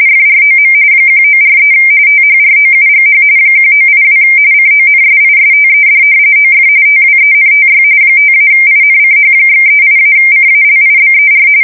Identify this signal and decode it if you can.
RTTY, about 45.45 baud. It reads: FROM 1943 THROUGH 1982 CONSISTED OF TWENTY 5-INCH/38-CALIBER
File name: fsk170 / battleshipnj